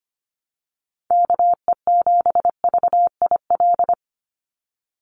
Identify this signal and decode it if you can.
Morse code, heard as a keyed tone, about 25 wpm, 700 Hz. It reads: KE7VIL
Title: USA Call Signs 25wpm